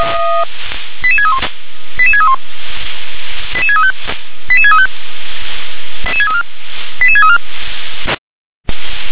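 Five-tone selective call (ZVEI), heard as quick runs of five tones, tones of about 70 ms. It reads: 88631 88631 88635 88635 88634
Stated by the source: ZVEI five-tone call received by SDR.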